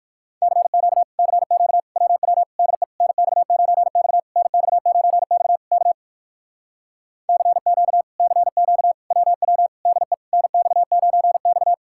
Morse code, heard as a keyed tone, about 63 wpm, 700 Hz. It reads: CQ CQ WW DE NX9X NX9X K CQ CQ WW DE NX9X